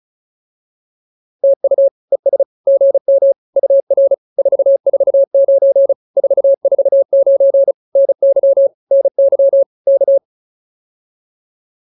Morse code, heard as a keyed tone, about 34 wpm, 560 Hz. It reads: TU ES GM UR 449 449 NY NY K
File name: POTA206 35